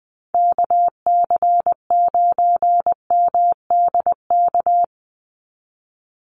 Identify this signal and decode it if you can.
Morse code, heard as a keyed tone, about 20 wpm, 700 Hz. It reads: KC9MDK
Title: USA Call Signs 20wpm